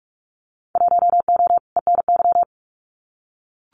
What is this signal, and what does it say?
Morse code, heard as a keyed tone, about 44 wpm, 700 Hz. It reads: 1O ENY
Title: Winter Field Day Exchanges 45wpm